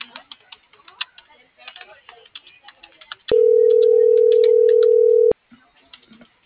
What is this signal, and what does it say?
Ringback tone (North America), 440 + 480 Hz.